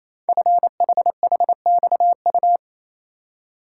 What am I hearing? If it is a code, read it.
Morse code, heard as a keyed tone, about 28 wpm, 700 Hz. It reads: FHHXU
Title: Code Groups - 5 Letters - 28wpm